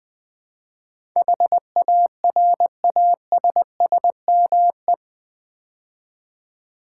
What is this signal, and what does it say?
Morse code, heard as a keyed tone, about 20 wpm, 700 Hz. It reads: HARASSME
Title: Build A Word - 4 Letters At A Time - 10-Letter Words - 20wpm